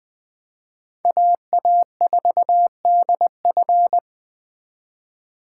Morse code, heard as a keyed tone, about 21 wpm, 700 Hz. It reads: AA4DF
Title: Speed Racing - USA 2x2 Call Signs 17wpm